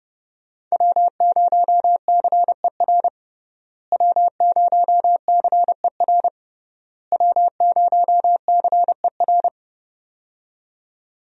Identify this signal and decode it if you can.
Morse code, heard as a keyed tone, about 32 wpm, 700 Hz. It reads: W0CER W0CER W0CER
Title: USA Call Signs - 5 Characters - Repeated 3x - 30wpm